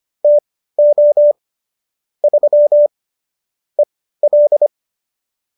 Morse code, heard as a keyed tone, about 25 wpm, spaced as for 15 wpm, 590 Hz. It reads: TO 3 EL